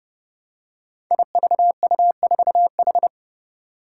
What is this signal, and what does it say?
Morse code, heard as a keyed tone, about 30 wpm, 700 Hz. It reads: IVU4H
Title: Five Challenging Letter-Number Combinations 30wpm